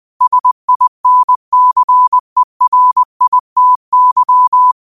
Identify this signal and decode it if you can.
Morse code, heard as a keyed tone, about 20 wpm, 1000 Hz. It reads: SINCERITY
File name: qotdfiles / 20wpm